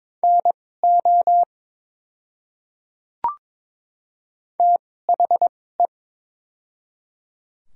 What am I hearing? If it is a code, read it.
Morse code, heard as a keyed tone, about 22 wpm, spaced as for 16 wpm, 700 Hz. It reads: NO THE
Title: 2x Character Spacing - Top 100 Words - 22wpm